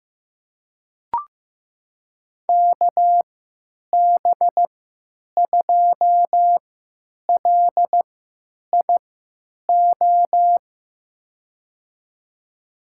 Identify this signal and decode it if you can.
Morse code, heard as a keyed tone, about 15 wpm, spaced as for 9 wpm, 700 Hz. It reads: KB2LIO
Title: USA Call Signs - 3x Character Spacing - 15wpm